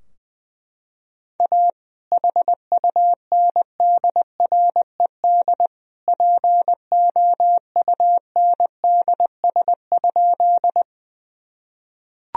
Morse code, heard as a keyed tone, about 20 wpm, 700 Hz. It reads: A HUNDRED POUNDS?
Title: Sentences from Top 1000 Words - Review Part 2 - 20wpm